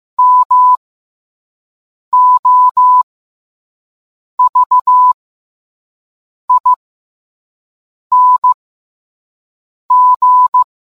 Morse code, heard as a keyed tone, about 15 wpm, spaced as for 5 wpm, 1000 Hz. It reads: MOVING